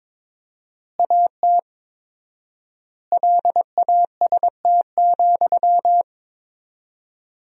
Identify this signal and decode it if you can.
Morse code, heard as a keyed tone, about 22 wpm, 700 Hz. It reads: AT LAST,
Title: 4x Word-Spacing - Sentences from Top 300 Words - 22wpm